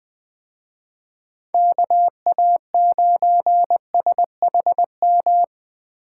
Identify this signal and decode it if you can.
Morse code, heard as a keyed tone, about 20 wpm, 700 Hz. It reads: KA9SHM
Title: Speed Racing - USA 2x3 Call Signs 17wpm